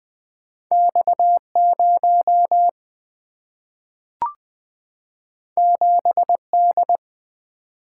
Morse code, heard as a keyed tone, about 20 wpm, 700 Hz. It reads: X 0 7 D
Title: Single-Letter Single-Number 20wpm